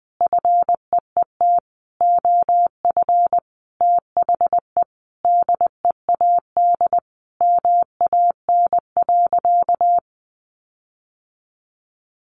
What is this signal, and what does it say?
Morse code, heard as a keyed tone, about 20 wpm, 700 Hz. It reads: FEET OF THE DEAD MAN.
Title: Sentences from Top 1500 Words 20wpm